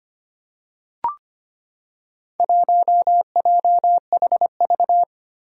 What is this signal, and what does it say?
Morse code, heard as a keyed tone, about 25 wpm, 700 Hz. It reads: 1JHV
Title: Four Challenging Letter-Number Combinations 25wpm